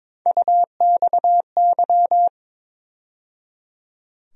Morse code, heard as a keyed tone, about 22 wpm, 700 Hz. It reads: UXY
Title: USA Call Sign Postfixes - 3 Letters 22wpm